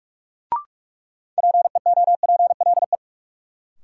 Morse code, heard as a keyed tone, about 47 wpm, 700 Hz. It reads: PEOPLE
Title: Sets of 2 Words - New Word Drill - 45wpm